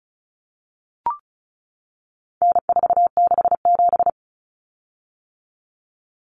Morse code, heard as a keyed tone, about 34 wpm, 700 Hz. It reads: N467
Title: Common Cut Numbers - 4-Digit 35wpm